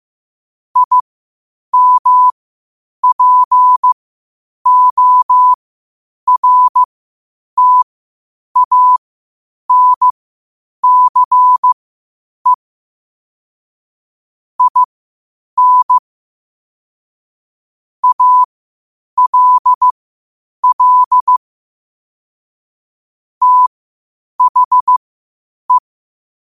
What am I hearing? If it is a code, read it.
Morse code, heard as a keyed tone, about 15 wpm, spaced as for 8 wpm, 1000 Hz. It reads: IMPORTANCE IN ALL THE
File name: qotdfiles / 8wpm